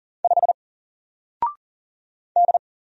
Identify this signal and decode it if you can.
Morse code, heard as a keyed tone, about 40 wpm, 700 Hz. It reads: 5 D
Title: Single Letter-Number 40wpm